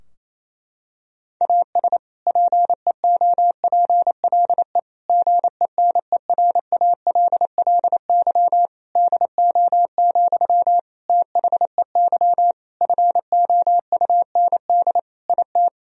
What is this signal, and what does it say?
Morse code, heard as a keyed tone, about 28 wpm, 700 Hz. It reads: AS PEOPLE GENERALLY DO, THEY FOUND IT
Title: Sentences from Top 1000 Words - Review Part 4 - 28wpm